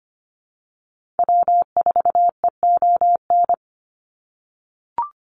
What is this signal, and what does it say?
Morse code, heard as a keyed tone, about 25 wpm, 690 Hz. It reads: W4EON
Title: NA SOTA Activator Callsigns 25wpm